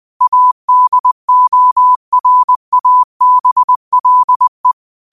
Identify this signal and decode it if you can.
Morse code, heard as a keyed tone, about 20 wpm, 1000 Hz. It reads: ADORABLE